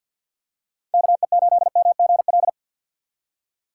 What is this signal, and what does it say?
Morse code, heard as a keyed tone, about 47 wpm, 700 Hz. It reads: KE8MGL
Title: USA Call Signs - 6 Characters - 4x Character Spacing - 50wpm